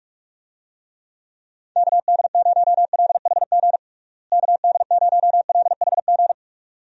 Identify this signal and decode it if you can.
Morse code, heard as a keyed tone, about 42 wpm, 700 Hz. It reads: KD0LHG KD0LHG
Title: USA Call Signs - 6 Characters - Repeated 2x - 45wpm